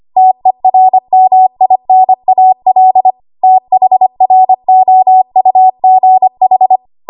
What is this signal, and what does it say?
Morse code, heard as a keyed tone, about 25 wpm, 750 Hz. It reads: TERMINAL THROUGH
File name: Archive / 25 WPM / 25WPM